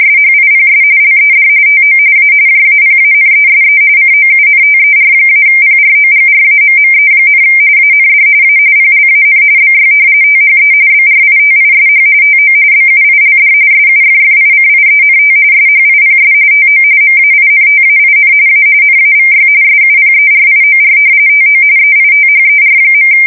RTTY, about 45.45 baud. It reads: SHIPS LED TO A JAPANESE COUNTER-PROGRAMME, AND A PLAN BY THE BRITISH TO BUILD SUFFICIENT SHIPS TO MAINTAIN A NAVY SUPERIOR TO EITHER.
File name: fsk170 / usnavy4